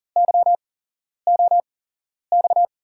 Morse code, heard as a keyed tone, about 38 wpm, spaced as for 11 wpm, 700 Hz. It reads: YOX
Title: USA Call Signs - 5 Characters - 8x Character Spacing - 40wpm